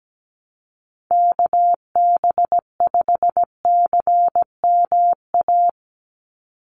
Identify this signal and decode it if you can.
Morse code, heard as a keyed tone, about 17 wpm, 700 Hz. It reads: KB5CMA
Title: USA Call Signs 17wpm